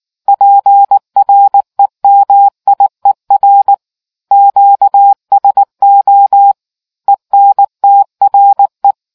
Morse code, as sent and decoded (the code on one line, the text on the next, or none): .--. .-. -- .. .-. / --.- ... --- / . -. - .-. .
PREMIER QSO ENTRE